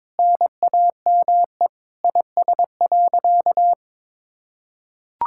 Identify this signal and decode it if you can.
Morse code, heard as a keyed tone, about 22 wpm, 700 Hz. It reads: NAME IS.
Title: Sentences from Top 1000 Words - Review Part 6 - 22wpm